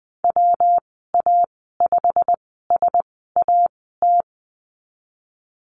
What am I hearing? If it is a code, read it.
Morse code, heard as a keyed tone, about 20 wpm, spaced as for 14 wpm, 700 Hz. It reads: WA5SAT
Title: 122 Lesson 20g - Number 3 - Call Signs 20wpm